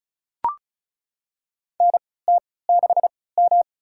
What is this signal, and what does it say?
Morse code, heard as a keyed tone, about 34 wpm, spaced as for 20 wpm, 700 Hz. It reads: NT6M
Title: USA Call Signs - 4 Characters - 3x Character Spacing - 35wpm